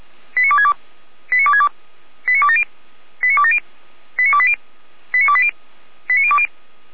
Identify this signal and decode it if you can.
Five-tone selective call (ZVEI), heard as quick runs of five tones, tones of about 70 ms. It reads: 89262 89262 89270 89270 89280 89280 89020